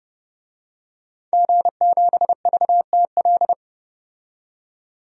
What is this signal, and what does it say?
Morse code, heard as a keyed tone, about 30 wpm, 700 Hz. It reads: G7VTL
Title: DXCC-100 Call Signs - 5 Characters - 30wpm